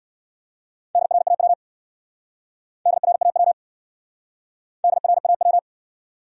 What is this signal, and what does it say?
Morse code, heard as a keyed tone, about 67 wpm, 700 Hz. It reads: DRAW DRAW DRAW